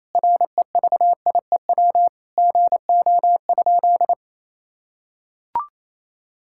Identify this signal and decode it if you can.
Morse code, heard as a keyed tone, about 28 wpm, 700 Hz. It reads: REVIEW GO?
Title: Sentences from Top 1000 Words - Review Part 6 - 28wpm